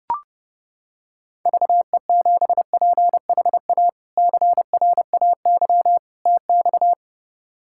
Morse code, heard as a keyed tone, about 30 wpm, 700 Hz. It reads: VE7PHA CRAY TX
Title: K1USN SST QSO Practice - 30wpm